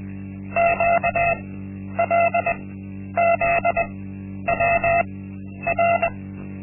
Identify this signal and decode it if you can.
Morse code, heard as a keyed tone, about 20 wpm, spaced as for 10 wpm, 680 Hz. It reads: QLZWR
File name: Linksmakalnis-1